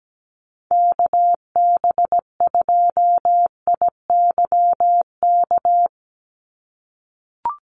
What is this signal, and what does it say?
Morse code, heard as a keyed tone, about 17 wpm, 700 Hz. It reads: KB2IYK
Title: USA Call Signs 17wpm